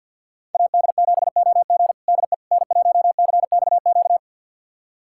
Morse code, heard as a keyed tone, about 51 wpm, 690 Hz. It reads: AD7OG DE N1CXQ
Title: USA Callsign DE Callsign - 5 Characters - 50wpm